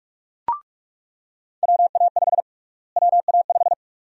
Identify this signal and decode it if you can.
Morse code, heard as a keyed tone, about 47 wpm, 700 Hz. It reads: WA5 WA5
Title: USA Call Sign Prefixes - Repeated 2x - 45wpm